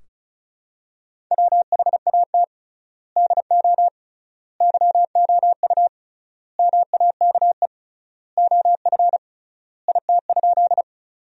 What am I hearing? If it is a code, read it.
Morse code, heard as a keyed tone, about 36 wpm, 700 Hz. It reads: WHAT DO YOU MAKE OF IT?
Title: 3x Word-Spacing - Sentences from Top 100 Words - 35wpm